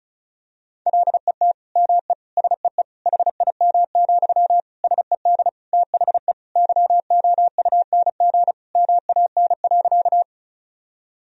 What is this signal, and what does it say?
Morse code, heard as a keyed tone, about 36 wpm, 700 Hz. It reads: LET ME SEE HIM, SED THE YOUNG MAN.
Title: Sentences from Top 400 Words and Abbreviations 35wpm